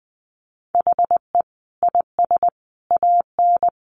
Morse code, heard as a keyed tone, about 20 wpm, 700 Hz. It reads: HE IS AN
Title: Sentences from Top 1000 Words 20wpm